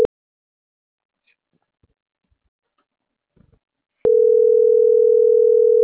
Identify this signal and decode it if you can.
Ringback tone (North America), 440 + 480 Hz.